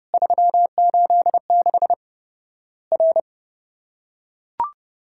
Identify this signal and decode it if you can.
Morse code, heard as a keyed tone, about 30 wpm, 700 Hz. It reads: 386 R
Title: Worked All Europe DX Contest - QTC Exchange - 30wpm